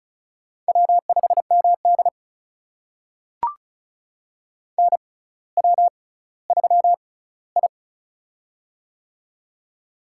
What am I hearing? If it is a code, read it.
Morse code, heard as a keyed tone, about 36 wpm, spaced as for 18 wpm, 700 Hz. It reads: W5MD NW3I
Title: USA Call Signs - 4 Characters - 6x Character Spacing - 35wpm